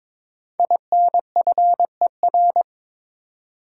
Morse code, heard as a keyed tone, about 22 wpm, 700 Hz. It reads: INFER
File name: 5 Letter Words - First 3 Letters Given - 22wpm